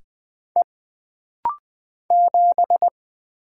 Morse code, heard as a keyed tone, about 20 wpm, 700 Hz. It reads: E 7
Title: Single Letter-Number - Rapid-Fire 20wpm